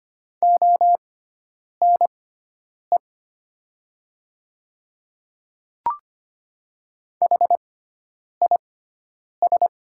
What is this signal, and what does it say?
Morse code, heard as a keyed tone, about 25 wpm, spaced as for 9 wpm, 700 Hz. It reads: ONE HIS
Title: 6x Character Spacing - Top 100 Words - 25wpm